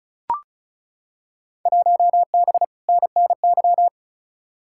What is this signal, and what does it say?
Morse code, heard as a keyed tone, about 36 wpm, 700 Hz. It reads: 1B NNY
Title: Field Day Exchanges Encore - 35wpm